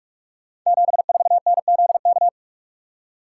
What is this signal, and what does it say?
Morse code, heard as a keyed tone, about 44 wpm, 700 Hz. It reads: 74NZK
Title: Code Groups - 5 Letter:Numbers - 45wpm